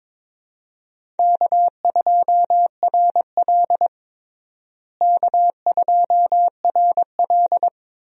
Morse code, heard as a keyed tone, about 22 wpm, 700 Hz. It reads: K2RL K2RL
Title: USA Call Signs - 4 Characters - Repeated 2x - 22wpm